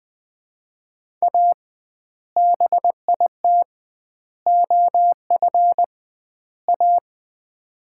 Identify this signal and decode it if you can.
Morse code, heard as a keyed tone, about 20 wpm, 700 Hz. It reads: A BIT OF A
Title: 2x Word-Spacing - Sets of 4 Words - 20wpm